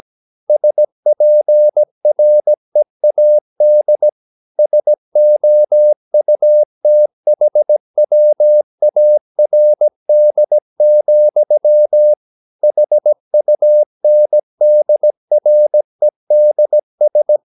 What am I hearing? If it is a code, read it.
Morse code, heard as a keyed tone, about 17 wpm, 600 Hz. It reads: SPREAD SOUTHWARD, HUNDREDS
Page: Romanen Never af Ken Follet på engelsk.